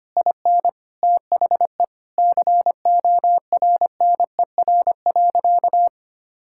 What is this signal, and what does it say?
Morse code, heard as a keyed tone, about 25 wpm, 700 Hz. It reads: IN THE CORNER.
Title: Sentences from Top 1600 Words 25wpm